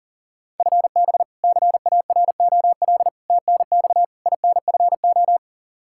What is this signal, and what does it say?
Morse code, heard as a keyed tone, about 37 wpm, 700 Hz. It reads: FB CAROL TNX INFO
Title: K1USN SST QSO Practice 40wpm